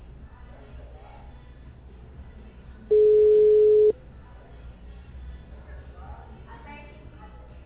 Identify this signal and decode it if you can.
Ringback tone (Europe, 425 Hz), with speech.